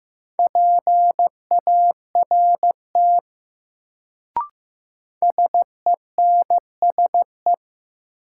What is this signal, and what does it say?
Morse code, heard as a keyed tone, about 15 wpm, 700 Hz. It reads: PART SENSE
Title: Sets of 2 Words - New Word Drill - 15wpm